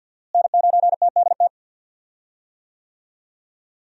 Morse code, heard as a keyed tone, about 48 wpm, 700 Hz. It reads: N9TDT
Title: USA Call Signs - 5 Characters - Repeated 3x - 50wpm